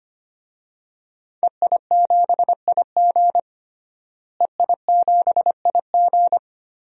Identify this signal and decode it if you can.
Morse code, heard as a keyed tone, about 25 wpm, 700 Hz. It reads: EI7IG EI7IG
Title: DXCC-100 Call Signs - 5 Characters - Repeated 2x - 25wpm